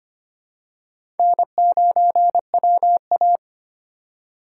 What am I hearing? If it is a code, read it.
Morse code, heard as a keyed tone, about 25 wpm, 700 Hz. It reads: N9WA
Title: USA Call Signs - 4 Characters - 4x Character Spacing - 25wpm